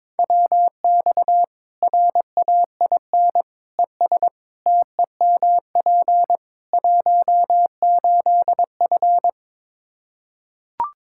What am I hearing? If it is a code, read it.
Morse code, heard as a keyed tone, about 22 wpm, 700 Hz. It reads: WX RAIN ES TEMP 18F
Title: K1USN SST QSO Practice 22wpm